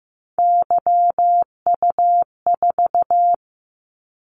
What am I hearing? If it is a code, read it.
Morse code, heard as a keyed tone, about 15 wpm, 700 Hz. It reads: YU4
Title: Callsign Prefixes for Top 25 DXCC - 15wpm